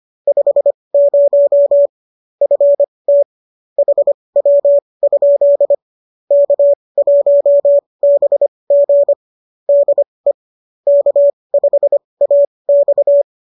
Morse code, heard as a keyed tone, about 25 wpm, spaced as for 20 wpm, 590 Hz. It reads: 50 FT HW? K1BG DE K5AX